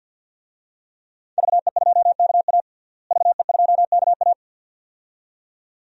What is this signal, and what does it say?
Morse code, heard as a keyed tone, about 52 wpm, 700 Hz. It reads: VE2KA VE2KA
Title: DXCC-100 Call Signs - 5 Characters - Repeated 2x - 50wpm